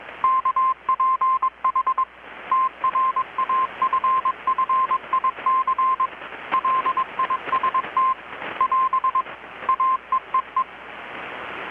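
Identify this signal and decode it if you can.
Morse code, heard as a keyed tone, about 21 wpm, 1030 Hz. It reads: KPH TRAFFIC LIST <AS> AEEE
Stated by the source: Morse from KPH's Night of Nights XX maritime radiotelegraphy event, received in Chiapas, Mexico about 3,800 km away on a Kenwood R-600 with a grounded 10 m long-wire antenna.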